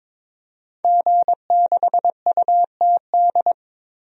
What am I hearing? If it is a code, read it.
Morse code, heard as a keyed tone, about 22 wpm, 700 Hz. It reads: G6UTD
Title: DXCC-100 Call Signs - 5 Characters - 22wpm